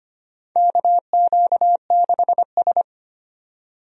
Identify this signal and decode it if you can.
Morse code, heard as a keyed tone, about 25 wpm, 700 Hz. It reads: KQ6S
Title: USA Call Signs - 4 Characters - 25wpm